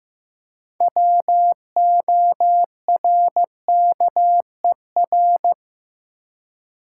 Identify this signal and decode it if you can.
Morse code, heard as a keyed tone, about 15 wpm, 700 Hz. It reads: WORKER
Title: Six Letter Words - 15wpm